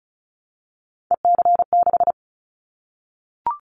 Morse code, heard as a keyed tone, about 36 wpm, 700 Hz. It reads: EC6
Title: Callsign Prefixes for Top 100 DXCC - 35wpm